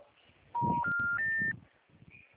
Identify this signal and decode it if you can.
Special information tone.
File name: out- -2013- - - .35404